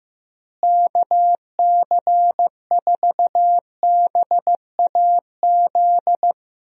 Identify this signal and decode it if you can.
Morse code, heard as a keyed tone, about 15 wpm, 700 Hz. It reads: KC4BAZ
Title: USA Call Signs - 6 Characters - Repeated 3x - 15wpm